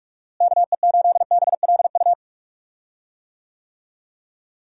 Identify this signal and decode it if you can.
Morse code, heard as a keyed tone, about 41 wpm, 700 Hz. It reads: KE8BLU
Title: USA Call Signs - 6 Characters - Repeated 3x - 45wpm